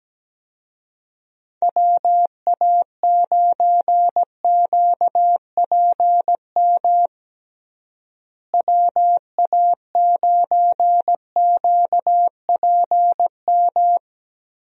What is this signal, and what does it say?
Morse code, heard as a keyed tone, about 17 wpm, 700 Hz. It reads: WA9QPM WA9QPM
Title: USA Call Signs - 6 Characters - Repeated 3x - 17wpm